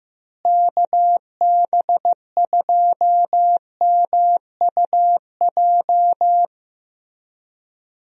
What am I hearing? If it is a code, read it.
Morse code, heard as a keyed tone, about 15 wpm, 700 Hz. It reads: KB2MUJ
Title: USA Call Signs - 6 Characters - Repeated 3x - 15wpm